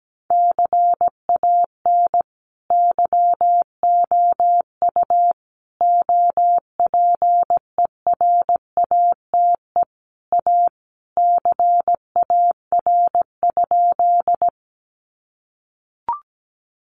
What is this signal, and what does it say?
Morse code, heard as a keyed tone, about 17 wpm, 700 Hz. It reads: CAN YOU OPERATE A CAR?
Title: Sentences from Top 1500 Words 17wpm